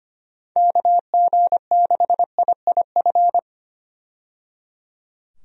Morse code, heard as a keyed tone, about 25 wpm, 700 Hz. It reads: KG6IIF
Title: USA Call Signs - 6 Characters - 25wpm